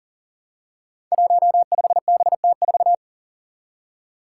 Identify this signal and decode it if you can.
Morse code, heard as a keyed tone, about 37 wpm, 700 Hz. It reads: 15BT4
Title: Five Challenging Letter-Number Combinations 40wpm